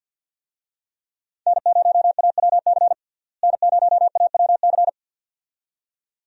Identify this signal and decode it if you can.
Morse code, heard as a keyed tone, about 50 wpm, 700 Hz. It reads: N0AWC N0AWC
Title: USA Call Signs - 5 Characters - Repeated 2x - 50wpm